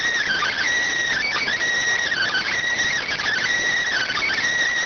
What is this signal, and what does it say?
Emergency Alert System (SAME) tones, heard as harsh buzzing bursts. It is heard from 0.2 to 4.3 s.